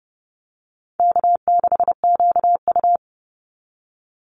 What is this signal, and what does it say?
Morse code, heard as a keyed tone, about 30 wpm, 700 Hz. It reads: K6QU
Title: Easy Call Signs Repeated 2x - 30wpm